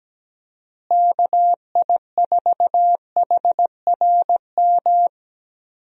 Morse code, heard as a keyed tone, about 17 wpm, 700 Hz. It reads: KI4HRM
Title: USA Call Signs - 6 Characters - Repeated 3x - 17wpm